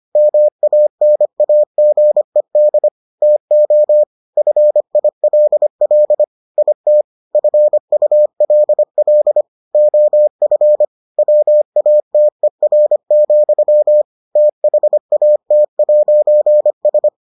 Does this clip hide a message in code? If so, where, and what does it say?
Yes: Morse code from 0.2 to 17.1 s, heard as a keyed tone, about 25 wpm, 600 Hz. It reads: MANAGED TO FILL IT FULL OF WATER, THAT'S